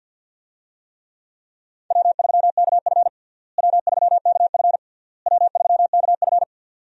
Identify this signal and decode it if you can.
Morse code, heard as a keyed tone, about 52 wpm, 700 Hz. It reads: W3KF W3KF W3KF
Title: USA Call Signs - 4 Characters - Repeated 3x - 50wpm